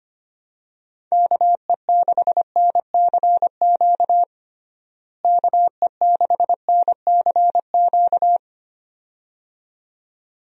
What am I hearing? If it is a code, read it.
Morse code, heard as a keyed tone, about 25 wpm, 700 Hz. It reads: KE6NCQ KE6NCQ
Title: USA Call Signs - 6 Characters - Repeated 2x - 25wpm